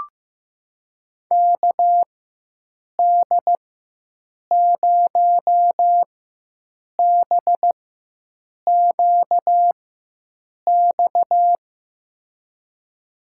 Morse code, heard as a keyed tone, about 16 wpm, spaced as for 7 wpm, 700 Hz. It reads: KD0BQX
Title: USA Call Signs - 4x Character Spacing - 15wpm